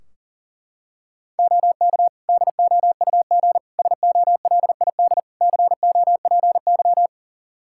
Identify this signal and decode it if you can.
Morse code, heard as a keyed tone, about 37 wpm, 700 Hz. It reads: OK DOUG SOLID COPY
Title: Practice QSO - Exchange 3 - More Info - 40wpm